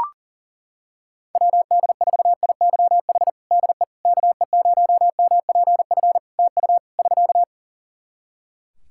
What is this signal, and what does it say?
Morse code, heard as a keyed tone, about 40 wpm, 700 Hz. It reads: WD4IYH DE KE0MPF TU <SK>
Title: Practice QSO - Exchange 4 - Ending - 40wpm